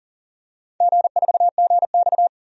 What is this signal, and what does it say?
Morse code, heard as a keyed tone, about 39 wpm, 700 Hz. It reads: G4GX
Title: DXCC-100 Call Signs - 4 Characters - 40wpm